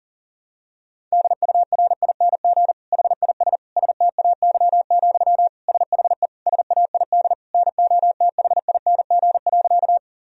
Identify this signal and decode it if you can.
Morse code, heard as a keyed tone, about 40 wpm, 700 Hz. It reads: DURING HIS STAY, SHE SAID NOTHING.
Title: Sentences from Top 1000 Words - Review Part 2 - 40wpm